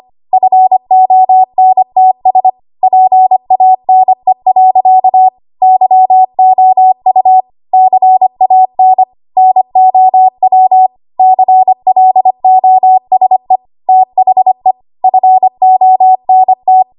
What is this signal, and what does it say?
Morse code, heard as a keyed tone, about 25 wpm, 750 Hz. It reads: FONTS PANE. YOU CAN NOW CLOSE THE FONT
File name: Archive / 25 WPM / 25WPM